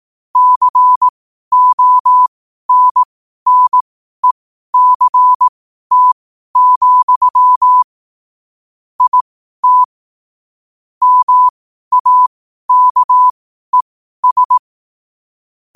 Morse code, heard as a keyed tone, about 18 wpm, spaced as for 12 wpm, 1000 Hz. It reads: CONNECT, IT MAKES
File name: qotdfiles / 12wpm